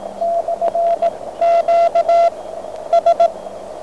Morse code, heard as a keyed tone, about 17 wpm, 690 Hz, with radio noise. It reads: / Q S